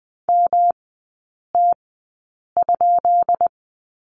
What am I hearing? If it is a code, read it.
Morse code, heard as a keyed tone, about 20 wpm, spaced as for 8 wpm, 690 Hz. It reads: MT?
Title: 191 Lesson 31e - Letter v - Three Characters 20wpm